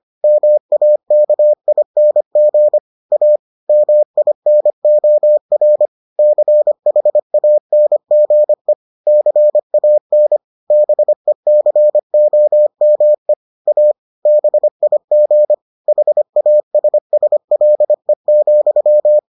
Morse code, heard as a keyed tone, about 25 wpm, 600 Hz. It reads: MAKING A MINOR CHANGE CAN BECOME A BIG HASSLE,